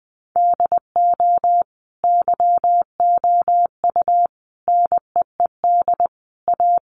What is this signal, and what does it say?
Morse code, heard as a keyed tone, about 20 wpm, 700 Hz. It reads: DO YOU NEED A
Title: Sentences from Top 1500 Words 20wpm